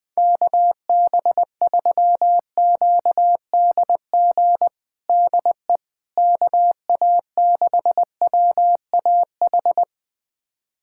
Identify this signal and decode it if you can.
Morse code, heard as a keyed tone, about 20 wpm, 700 Hz. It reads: KB3QDG DE KA6WAH
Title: USA Callsign DE Callsign - 6 Characters - 20wpm